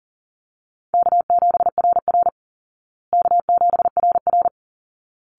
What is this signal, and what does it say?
Morse code, heard as a keyed tone, about 40 wpm, 700 Hz. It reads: K7RR K7RR
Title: Call Signs Repeated 3x - 40wpm